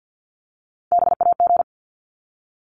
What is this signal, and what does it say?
Morse code, heard as a keyed tone, about 50 wpm, 700 Hz. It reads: BAG
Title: Top 1501-1750 Words 50wpm